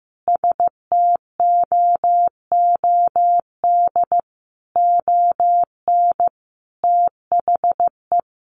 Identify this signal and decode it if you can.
Morse code, heard as a keyed tone, about 15 wpm, 700 Hz. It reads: STOOD ON THE
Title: Sentences from Top 1200 Words 15wpm